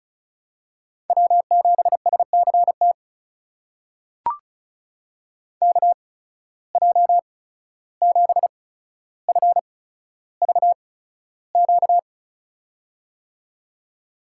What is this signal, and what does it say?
Morse code, heard as a keyed tone, about 36 wpm, spaced as for 14 wpm, 700 Hz. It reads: W7SCT KJ7FVQ
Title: USA Call Signs - 8x Character Spacing - 35wpm